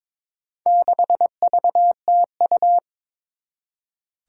Morse code, heard as a keyed tone, about 22 wpm, 700 Hz. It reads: 6VTU
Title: Four Challenging Letter-Number Combinations 22wpm